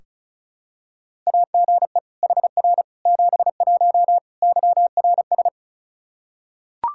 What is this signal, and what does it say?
Morse code, heard as a keyed tone, about 36 wpm, 700 Hz. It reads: AGE HR 71 YRS
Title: Practice QSO - Exchange 3 - More Info - 35wpm